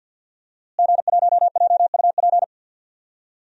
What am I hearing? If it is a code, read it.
Morse code, heard as a keyed tone, about 52 wpm, 700 Hz. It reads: G1JUP
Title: DXCC-100 Call Signs - 5 Characters - Repeated 3x - 50wpm